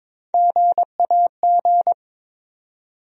Morse code, heard as a keyed tone, about 22 wpm, 700 Hz. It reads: GAG
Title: USA Call Sign Postfixes - 3 Letters 22wpm